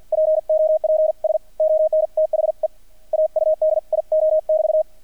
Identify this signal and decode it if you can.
Morse code, heard as a keyed tone, about 48 wpm, 630 Hz. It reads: POPS ON THE AUDIO <BT>
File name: archived / CRSnet